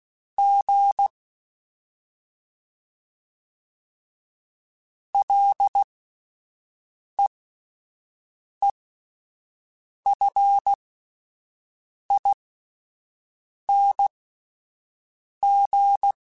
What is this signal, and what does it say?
Morse code, heard as a keyed tone, about 16 wpm, spaced as for 5 wpm, 800 Hz. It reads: G LEEFING
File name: Ls / Single L m3s / Single L Prefixe 3x2 Twice @5